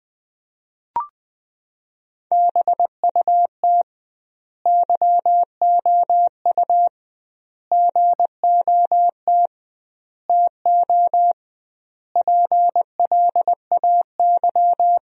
Morse code, heard as a keyed tone, about 20 wpm, 700 Hz. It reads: BUT YOU GOT TO PLAY
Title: 2x Word-Spacing - Sentences from Top 300 Words - 20wpm